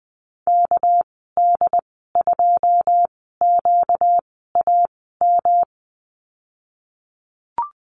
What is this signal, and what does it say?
Morse code, heard as a keyed tone, about 20 wpm, 700 Hz. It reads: K D 2 Q A M
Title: 238 Lesson 38g - Letter q - Call Signs 20wpm